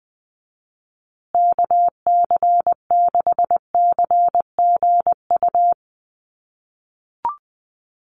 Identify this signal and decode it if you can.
Morse code, heard as a keyed tone, about 20 wpm, 700 Hz. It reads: KC6CGU
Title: USA Call Signs Repeated 2x 20wpm